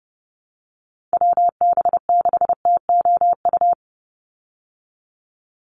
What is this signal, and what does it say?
Morse code, heard as a keyed tone, about 30 wpm, 700 Hz. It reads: WB6TOU
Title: Call Signs Repeated 2x - 30wpm